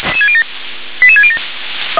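Five-tone selective call (ZVEI), heard as quick runs of five tones, tones of about 70 ms. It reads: 88607
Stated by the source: ZVEI five-tone call 88756 received by SDR.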